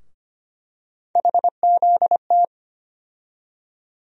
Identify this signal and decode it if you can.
Morse code, heard as a keyed tone, about 25 wpm, 700 Hz. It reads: HZT
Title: USA Call Sign Postfixes - 3 Letters 25wpm